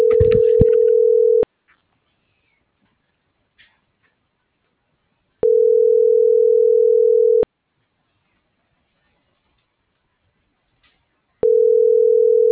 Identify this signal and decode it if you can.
Ringback tone (North America), 440 + 480 Hz.
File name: CENTRAL SAGOT / out danae.bsagot